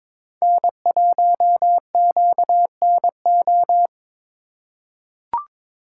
Morse code, heard as a keyed tone, about 23 wpm, 700 Hz. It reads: N1QNO
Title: USA Call Signs - 4x Character Spacing - 22wpm